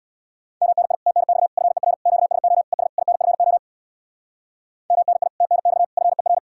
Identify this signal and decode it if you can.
Morse code, heard as a keyed tone, about 64 wpm, 700 Hz. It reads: MNI TNX FER QSO ES INFO MNI TNX FER